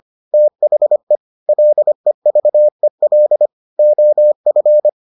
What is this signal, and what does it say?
Morse code, heard as a keyed tone, about 25 wpm, 600 Hz. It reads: THE LEVEL OF